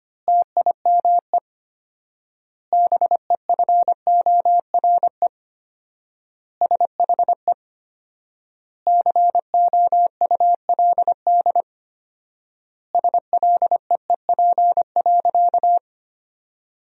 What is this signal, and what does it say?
Morse code, heard as a keyed tone, about 25 wpm, 700 Hz. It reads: TIME BEFORE SHE COULD SLEEP.
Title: 4x Word-Spacing - Sentences from Top 500 Words - 25wpm